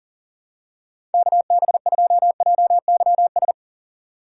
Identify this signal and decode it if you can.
Morse code, heard as a keyed tone, about 40 wpm, 700 Hz. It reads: KB2JYS
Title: USA Call Signs - 6 Characters - 8x Character Spacing - 40wpm